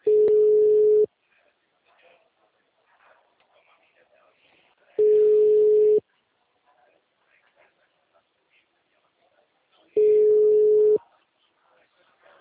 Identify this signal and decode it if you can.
Ringback tone (Europe, 425 Hz), with speech.